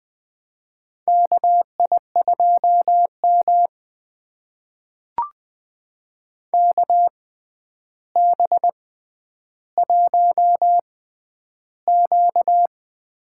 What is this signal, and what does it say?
Morse code, heard as a keyed tone, about 20 wpm, spaced as for 10 wpm, 700 Hz. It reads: KI2M KB1Q
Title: USA Call Signs - 6x Character Spacing - 20wpm